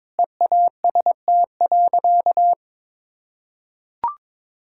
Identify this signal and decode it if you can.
Morse code, heard as a keyed tone, about 22 wpm, 700 Hz. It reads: EAST.
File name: practice / 22wpm / Sentences with 3-Word Phrases - 22wpm